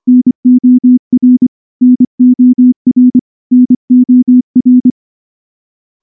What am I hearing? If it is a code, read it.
Morse code, heard as a keyed tone, about 24 wpm, 250 Hz. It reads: NOR NOR NOR
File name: morse / morse-training-25WPM